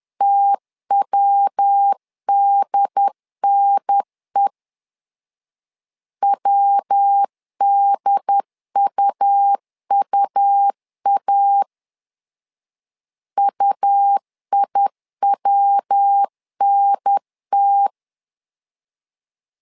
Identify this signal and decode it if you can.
Morse code, heard as a keyed tone, about 10 wpm, 800 Hz. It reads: TWDNE WDUUA UIWNT
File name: LEZIONI / 3 MWRO / 3 MW 10wpm 1959